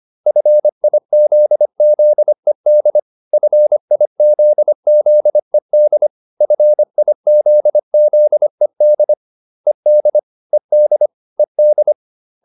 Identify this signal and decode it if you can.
Morse code, heard as a keyed tone, about 25 wpm, 600 Hz. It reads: FIZZED FIZZED FIZZED ED ED ED